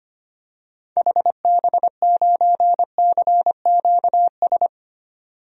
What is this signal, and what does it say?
Morse code, heard as a keyed tone, about 25 wpm, 700 Hz. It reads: HB9CQS
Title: DXCC-100 Call Signs - 6 Characters - 25wpm